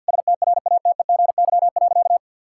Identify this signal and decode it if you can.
Morse code, heard as a keyed tone, about 50 wpm, 700 Hz. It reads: STRATEGY.
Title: Sentences from Top 1000 Words - Review Part 6 - 50wpm